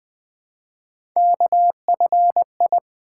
Morse code, heard as a keyed tone, about 20 wpm, 700 Hz. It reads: KFI
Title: Three Letter-Number Combinations - 20wpm